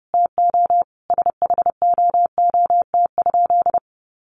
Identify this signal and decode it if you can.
Morse code, heard as a keyed tone, about 30 wpm, 700 Hz. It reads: TO SHOOT?
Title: A Sound Of Thunder 30wpm